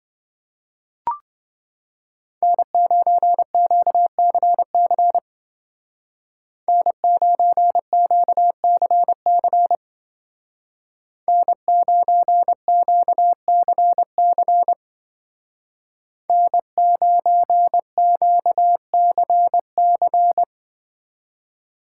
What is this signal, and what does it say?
Morse code, heard as a keyed tone, about 24 wpm, 700 Hz. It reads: N9QCC N9QCC N9QCC N9QCC
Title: Speed Racing - USA 1x3 Call Signs 20wpm